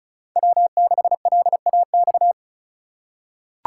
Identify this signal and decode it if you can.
Morse code, heard as a keyed tone, about 36 wpm, 700 Hz. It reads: W6LAX
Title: K1USN SST QSO Practice - 35wpm